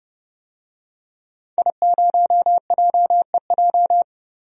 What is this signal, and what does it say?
Morse code, heard as a keyed tone, about 30 wpm, 700 Hz. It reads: I0JEJ
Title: DXCC-100 Call Signs - 5 Characters - Repeated 2x - 30wpm